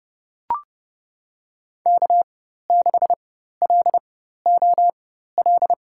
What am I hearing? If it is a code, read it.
Morse code, heard as a keyed tone, about 31 wpm, spaced as for 14 wpm, 700 Hz. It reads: K6LOL
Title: USA Call Signs - 4x Character Spacing - 30wpm